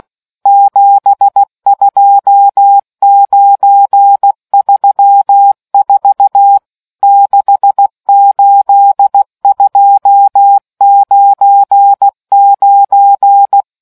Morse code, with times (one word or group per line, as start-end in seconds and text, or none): 0.5-6.6 72934
7.0-13.6 68299